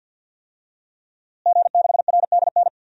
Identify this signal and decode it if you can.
Morse code, heard as a keyed tone, about 47 wpm, 700 Hz. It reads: G6RDN
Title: DXCC-100 Call Signs - 5 Characters - Repeated 2x - 50wpm